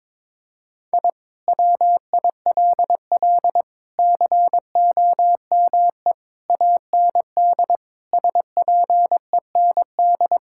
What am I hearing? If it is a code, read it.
Morse code, heard as a keyed tone, about 22 wpm, 700 Hz. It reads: I WILL COME AND SPEND
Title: Sentences from Top 1000 Words - Review Part 5 - 22wpm